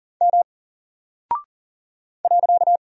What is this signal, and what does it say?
Morse code, heard as a keyed tone, about 40 wpm, 700 Hz. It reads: M .
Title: Single Character (Letter:Number:Punctuation:Prosign) Alt 40wpm